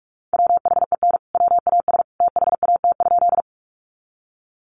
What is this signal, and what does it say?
Morse code, heard as a keyed tone, about 44 wpm, 700 Hz. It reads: WHEN WAS THAT?
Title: Sentences from Top 100 Words 45wpm